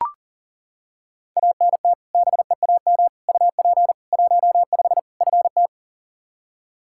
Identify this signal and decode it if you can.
Morse code, heard as a keyed tone, about 37 wpm, 700 Hz. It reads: ANT BEAM UP 15 FT
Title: Practice QSO - Exchange 2 - Info - 40wpm